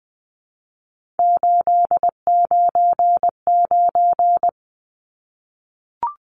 Morse code, heard as a keyed tone, about 20 wpm, 700 Hz. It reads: 899
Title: Numbers - 3-Digit 20wpm